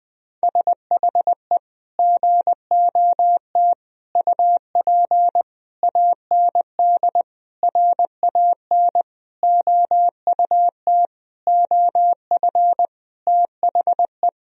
Morse code, heard as a keyed tone, about 19 wpm, 700 Hz. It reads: SHE GOT UP AND RAN OUT OF THE
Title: Sentences from Top 1000 Words - Review Part 3 - 20wpm